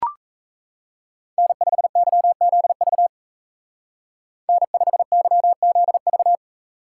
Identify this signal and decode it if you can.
Morse code, heard as a keyed tone, about 33 wpm, 700 Hz. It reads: N5YZV N5YZV
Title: Speed Racing - USA 1x3 Call Signs 28wpm